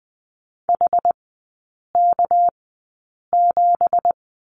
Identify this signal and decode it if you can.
Morse code, heard as a keyed tone, about 20 wpm, spaced as for 8 wpm, 700 Hz. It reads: HK7
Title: 218 Lesson 35e - Number 0 - Three Characters 20wpm